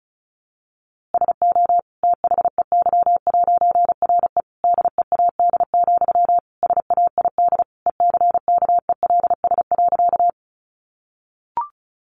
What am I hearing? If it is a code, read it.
Morse code, heard as a keyed tone, about 34 wpm, 700 Hz. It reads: SO THEY'RE DEAD, SAID ECKELS.
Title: A Sound Of Thunder 35wpm